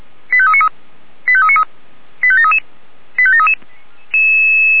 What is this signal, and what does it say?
Five-tone selective call (ZVEI), heard as quick runs of five tones, tones of about 70 ms. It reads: 85393 85393 85733 85733